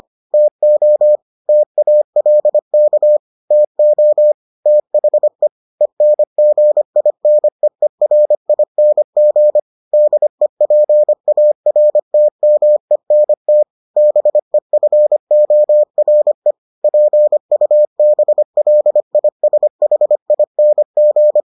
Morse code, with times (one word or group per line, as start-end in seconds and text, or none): 0.3-1.2 TO
1.5-3.2 TALK
3.5-4.3 TO
4.7-5.5 THE
5.8-9.6 ENGINEERING
9.9-13.6 DEPARTMENT
14.0-16.5 BEFORE
16.8-21.4 PUBLISHING